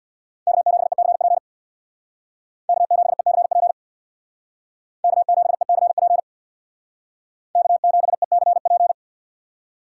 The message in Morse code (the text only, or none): K7ECP K7ECP K7ECP K7ECP